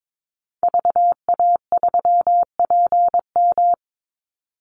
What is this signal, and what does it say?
Morse code, heard as a keyed tone, about 22 wpm, 700 Hz. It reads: VA3PM
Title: Call Signs - 22wpm